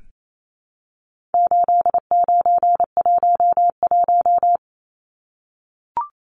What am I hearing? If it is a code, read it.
Morse code, heard as a keyed tone, about 28 wpm, 700 Hz. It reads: 8911
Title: Numbers - 4-Digit 28wpm